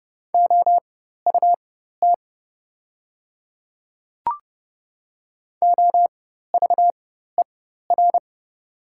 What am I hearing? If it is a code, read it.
Morse code, heard as a keyed tone, about 30 wpm, spaced as for 14 wpm, 700 Hz. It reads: OUT OVER
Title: 4x Character Spacing - Top 200 Words - 30wpm